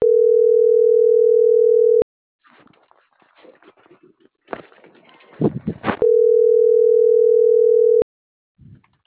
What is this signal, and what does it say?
Ringback tone (North America), 440 + 480 Hz.
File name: CENTRAL MAERSK / out alan.vicente